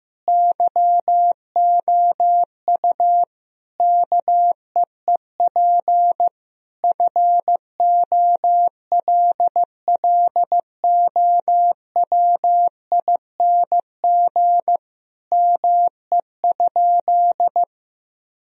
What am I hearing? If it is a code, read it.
Morse code, heard as a keyed tone, about 15 wpm, 700 Hz. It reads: YOU KEEP FOLLOWING ME?
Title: Sentences from Top 1000 Words - Review Part 4 - 15wpm